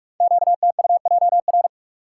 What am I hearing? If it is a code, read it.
Morse code, heard as a keyed tone, about 44 wpm, 700 Hz. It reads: QTUJR
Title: Code Groups - 5 Letters - 45wpm